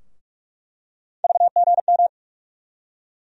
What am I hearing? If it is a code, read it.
Morse code, heard as a keyed tone, about 43 wpm, 700 Hz. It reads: VGM
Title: USA Call Sign Postfixes - 3 Letters 45wpm